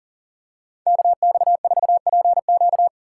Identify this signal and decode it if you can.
Morse code, heard as a keyed tone, about 37 wpm, 700 Hz. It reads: KX4PQ
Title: K1USN SST QSO Practice - 40wpm